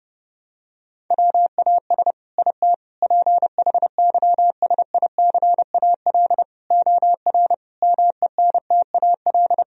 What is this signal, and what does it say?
Morse code, heard as a keyed tone, about 30 wpm, 700 Hz. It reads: WAS IT PHYSICAL OR MENTAL
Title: Sentences from Top 1000 Words - Review Part 4 - 30wpm